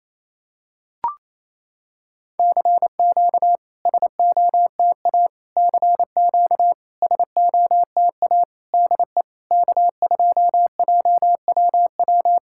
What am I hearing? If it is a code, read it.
Morse code, heard as a keyed tone, about 28 wpm, 700 Hz. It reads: CQ SOTA CQ SOTA DE K2JWW
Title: SOTA QSO Practice 28wpm